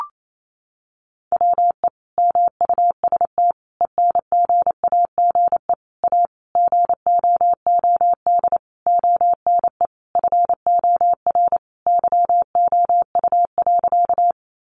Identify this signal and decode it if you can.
Morse code, heard as a keyed tone, about 28 wpm, 700 Hz. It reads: WE MUST ENGAGE A GOOD ONE FOR YOU.
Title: Sentences from Top 1500 Words 28wpm